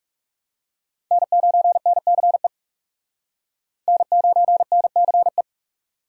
Morse code, heard as a keyed tone, about 37 wpm, 700 Hz. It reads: N9NCE N9NCE
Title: Speed Racing - USA 1x3 Call Signs 30wpm